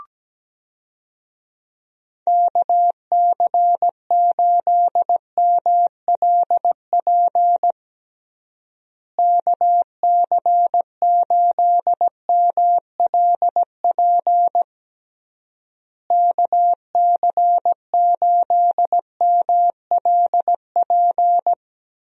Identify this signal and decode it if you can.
Morse code, heard as a keyed tone, about 17 wpm, 700 Hz. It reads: KC8MLP KC8MLP KC8MLP
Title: USA Call Signs - 6 Characters - Repeated 3x - 17wpm